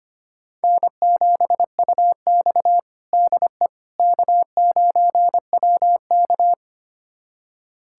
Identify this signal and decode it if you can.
Morse code, heard as a keyed tone, about 25 wpm, 700 Hz. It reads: N7UX DE K9WK
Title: USA Callsign DE Callsign - 4 Characters - 25wpm